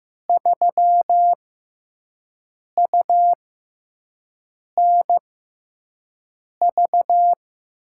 Morse code, heard as a keyed tone, about 15 wpm, spaced as for 5 wpm, 700 Hz. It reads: 3UNV
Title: USA Call Signs - 6 Characters - 6x Character Spacing - 15wpm